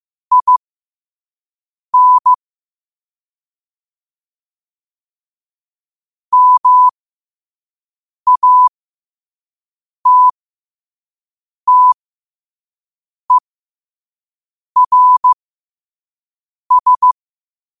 Morse code, heard as a keyed tone, about 15 wpm, spaced as for 5 wpm, 1000 Hz. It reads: IN MATTERS